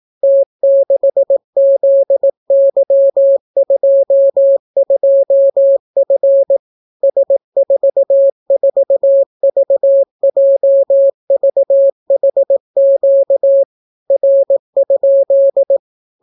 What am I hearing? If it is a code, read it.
Morse code, heard as a keyed tone, about 18 wpm, 550 Hz. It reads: T6ZY22F S44VJVHQ R?